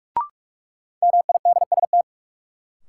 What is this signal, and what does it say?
Morse code, heard as a keyed tone, about 42 wpm, 700 Hz. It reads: MIDST
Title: Sets of 4 Words - New Word Drill - 45wpm